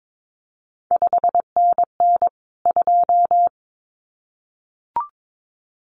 Morse code, heard as a keyed tone, about 22 wpm, 700 Hz. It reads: 5NN 2
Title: CQ WW DX Contest Exchange Encore - 22wpm